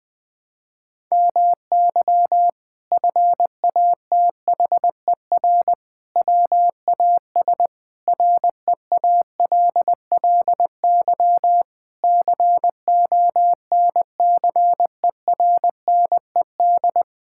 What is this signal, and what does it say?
Morse code, heard as a keyed tone, about 21 wpm, 700 Hz. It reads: MY FATHER WAS REALLY CONCERNED
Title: Sentences from Top 1000 Words - Review Part 6 - 20wpm